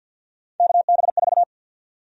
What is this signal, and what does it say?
Morse code, heard as a keyed tone, about 52 wpm, 700 Hz. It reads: KB4
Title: USA Call Sign Prefixes - Repeated 2x - 50wpm